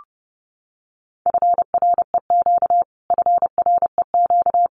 Morse code, heard as a keyed tone, about 30 wpm, 700 Hz. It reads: FREQ FREQ
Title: Abbreviations Repeated 3x - 30wpm